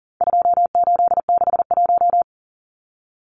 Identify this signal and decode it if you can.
Morse code, heard as a keyed tone, about 40 wpm, 700 Hz. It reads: 2861
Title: Numbers - 4-Digit 40wpm